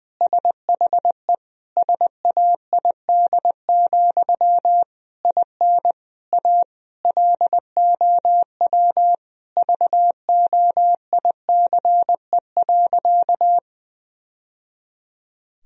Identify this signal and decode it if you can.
Morse code, heard as a keyed tone, about 21 wpm, 700 Hz. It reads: SHE SAID, IN A LOW VOICE.
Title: Sentences from Top 1000 Words - Review Part 1 - 20wpm